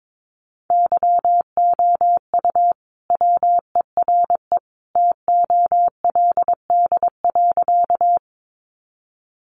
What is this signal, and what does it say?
Morse code, heard as a keyed tone, about 22 wpm, 700 Hz. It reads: YOU WERE TOLD.
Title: Sentences from Top 400 Words 22wpm